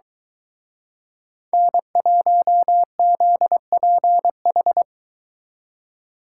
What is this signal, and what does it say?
Morse code, heard as a keyed tone, about 23 wpm, 700 Hz. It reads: N1ZPH